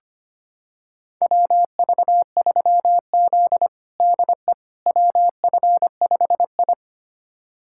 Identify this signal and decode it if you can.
Morse code, heard as a keyed tone, about 25 wpm, 700 Hz. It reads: WV3Z DE WF5I
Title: USA Callsign DE Callsign - 4 Characters - 25wpm